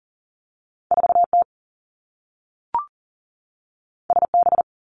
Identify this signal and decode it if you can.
Morse code, heard as a keyed tone, about 40 wpm, 700 Hz. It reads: 4 T S B
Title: Two Challenging Letter-Number Combinations 40wpm